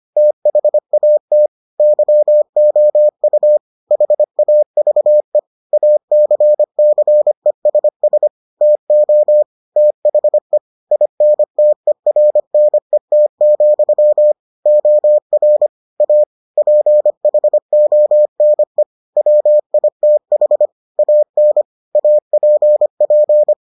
Morse code, heard as a keyed tone, about 25 wpm, 600 Hz. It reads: THAT YOU HAVE ACCESS TO THE INTERNET, OR A PHONE WITH AN APP